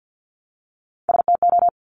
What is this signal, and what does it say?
Morse code, heard as a keyed tone, about 50 wpm, 700 Hz. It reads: STO
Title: Top Three Letter Combinations in Words 50wpm